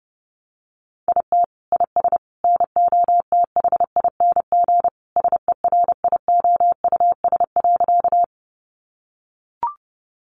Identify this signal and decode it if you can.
Morse code, heard as a keyed tone, about 30 wpm, 700 Hz. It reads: IT IS NOTHING SERIOUS.
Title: Sentences from Top 1100 Words 30wpm